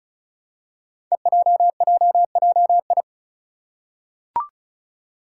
Morse code, heard as a keyed tone, about 36 wpm, 700 Hz. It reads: EJJJI
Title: Five Challenging Letter-Number Combinations 35wpm